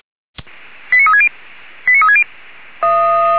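Five-tone selective call (ZVEI), heard as quick runs of five tones, tones of about 70 ms. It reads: 89360 89360
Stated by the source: ZVEI five-tone selective call received with an SDR.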